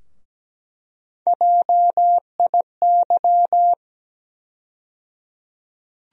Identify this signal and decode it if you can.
Morse code, heard as a keyed tone, about 17 wpm, 700 Hz. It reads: JIY
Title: USA Call Sign Postfixes - 3 Letters 17wpm